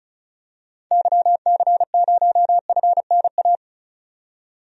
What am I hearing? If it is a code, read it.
Morse code, heard as a keyed tone, about 36 wpm, 700 Hz. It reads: YC0FNA
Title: DXCC-100 Call Signs - 6 Characters - 35wpm